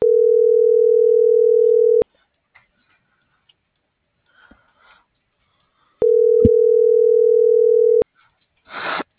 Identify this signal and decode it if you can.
Ringback tone (North America), 440 + 480 Hz.